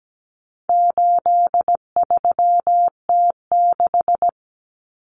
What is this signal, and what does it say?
Morse code, heard as a keyed tone, about 17 wpm, 700 Hz. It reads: 83T6
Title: Common Cut Numbers - 4-Digit 17wpm